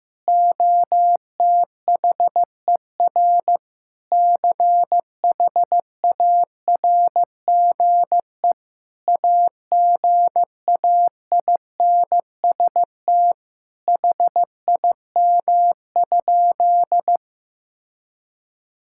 Morse code, heard as a keyed tone, about 15 wpm, 700 Hz. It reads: OTHER CHARGE AGAINST HIM?
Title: Sentences from Top 1000 Words - Review Part 6 - 15wpm